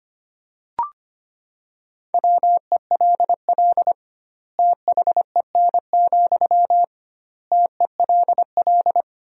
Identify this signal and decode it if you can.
Morse code, heard as a keyed tone, about 25 wpm, 700 Hz. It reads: WELL THEN, TELL
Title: 2x Word-Spacing - Sentences from Top 200 Words - 25wpm